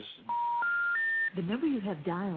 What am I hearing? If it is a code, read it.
Special information tone, with speech.